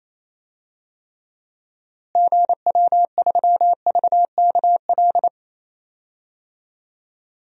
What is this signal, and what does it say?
Morse code, heard as a keyed tone, about 28 wpm, 700 Hz. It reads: GW3VKL
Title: DXCC-100 Call Signs - 6 Characters - Repeated 3x - 28wpm